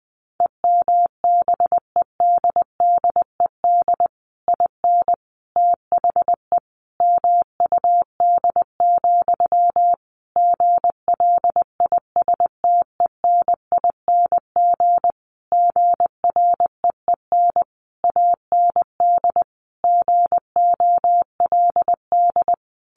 Morse code, heard as a keyed tone, about 20 wpm, 700 Hz. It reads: EMBEDDED IN THE MUD, GLISTENING GREEN AND GOLD
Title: A Sound Of Thunder 20wpm